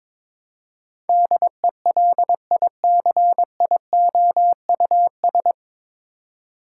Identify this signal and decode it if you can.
Morse code, heard as a keyed tone, about 21 wpm, 700 Hz. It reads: DELICIOUS
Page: Morse Code Ninja practice 22wpm